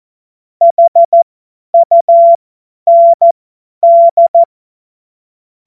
Morse code, heard as a keyed tone, about 14 wpm, spaced as for 10 wpm, 680 Hz. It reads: HUND
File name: Morsekurs / Witze / Witz 16. August 2022 50 F70